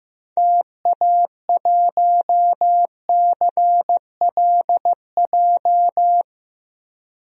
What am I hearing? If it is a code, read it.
Morse code, heard as a keyed tone, about 15 wpm, 700 Hz. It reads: TA1CLJ
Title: DXCC-100 Call Signs - 6 Characters - 15wpm